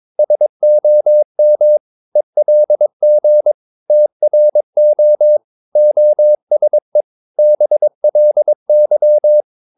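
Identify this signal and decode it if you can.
Morse code, heard as a keyed tone, about 22 wpm, 600 Hz. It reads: SOM ELG TRO OSE BLY
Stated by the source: Morse practice of short Danish words at 22 wpm, from CW med Gnister.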